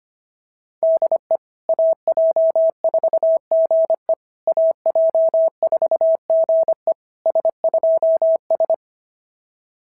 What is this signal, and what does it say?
Morse code, heard as a keyed tone, about 25 wpm, 650 Hz. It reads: DE AJ4GE AJ4GE S2S
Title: SOTA QSO Practice 25wpm